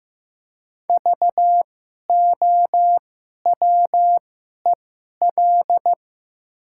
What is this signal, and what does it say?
Morse code, heard as a keyed tone, about 15 wpm, spaced as for 11 wpm, 700 Hz. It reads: VOWEL
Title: 2x Character Spacing - Top 500 Words - 15wpm